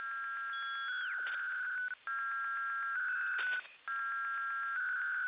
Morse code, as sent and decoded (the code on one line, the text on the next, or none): none